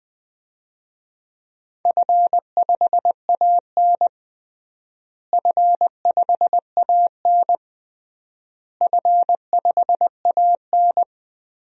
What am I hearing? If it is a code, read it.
Morse code, heard as a keyed tone, about 21 wpm, 700 Hz. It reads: F5AN F5AN F5AN
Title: DXCC-100 Call Signs - 4 Characters - Repeated 3x - 20wpm